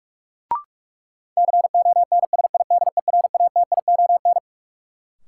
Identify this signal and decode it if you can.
Morse code, heard as a keyed tone, about 46 wpm, 700 Hz. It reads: CONSIDERATION
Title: Sentences from Top 1500 Words - New Word Drill Alt - 45wpm